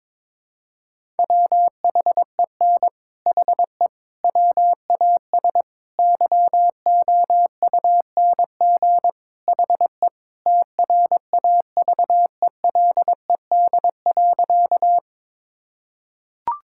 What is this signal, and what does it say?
Morse code, heard as a keyed tone, about 22 wpm, 700 Hz. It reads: WHEN HE WAS YOUNG HE TRAVELED.
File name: practice / 22wpm / Sentences with 3-Word Phrases - 22wpm